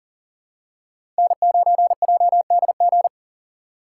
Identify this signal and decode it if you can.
Morse code, heard as a keyed tone, about 40 wpm, 700 Hz. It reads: N9JDG
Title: USA Call Signs - 6x Character Spacing - 40wpm